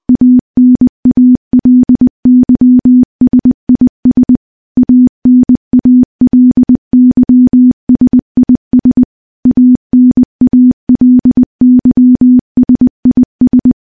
Morse code, heard as a keyed tone, about 21 wpm, 250 Hz. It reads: ANALYSIS ANALYSIS ANALYSIS
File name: morse-training-20WPM